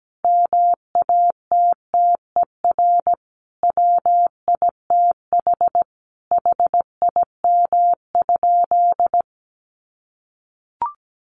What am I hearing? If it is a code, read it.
Morse code, heard as a keyed tone, about 17 wpm, 700 Hz. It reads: MATTER WITH HIM?
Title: Sentences from Top 700 Words 17wpm